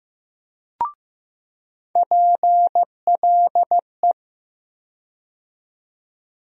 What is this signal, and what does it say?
Morse code, heard as a keyed tone, about 15 wpm, 700 Hz. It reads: PLE
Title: USA Call Sign Postfixes - 3 Letters 15wpm